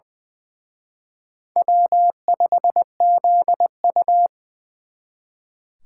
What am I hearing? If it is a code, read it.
Morse code, heard as a keyed tone, about 24 wpm, 700 Hz. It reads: W5ZU
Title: Speed Racing - USA Call Signs 20wpm